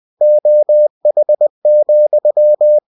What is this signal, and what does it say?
Morse code, heard as a keyed tone, about 20 wpm, 600 Hz. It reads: OH,
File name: cw-books / winnie-the-pooh / winnie-20wpm-page-0042